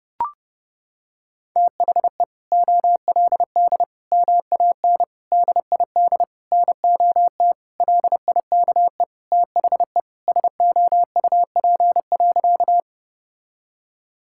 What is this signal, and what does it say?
Morse code, heard as a keyed tone, about 30 wpm, 700 Hz. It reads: THE OLD MAN DID NOT LIKE THE SOUP.
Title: Sentences from Top 1000 Words - Review Part 6 - 30wpm